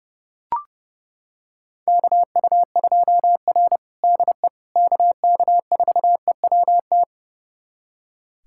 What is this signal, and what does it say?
Morse code, heard as a keyed tone, about 30 wpm, 700 Hz. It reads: KU2R DE KK4EWT
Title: USA Callsign DE Callsign 30wpm